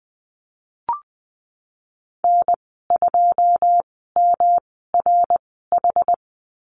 Morse code, heard as a keyed tone, about 20 wpm, spaced as for 15 wpm, 700 Hz. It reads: N2MRH
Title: 147 Lesson 24g - Letter y - Call Signs 20wpm